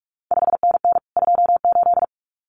Morse code, heard as a keyed tone, about 46 wpm, 700 Hz. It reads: 5NN 28
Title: CQ WW DX Contest Exchange Encore - 45wpm